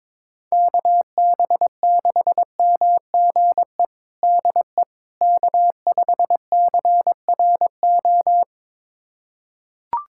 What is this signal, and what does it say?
Morse code, heard as a keyed tone, about 22 wpm, 700 Hz. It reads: KB6MGE DE K5CRO
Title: USA Callsign DE Callsign 22wpm